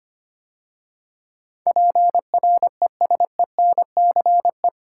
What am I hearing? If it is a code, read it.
Morse code, heard as a keyed tone, about 25 wpm, 700 Hz. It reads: PRESENCE
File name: 8 Letter Words - First 3 Letters Given - 25wpm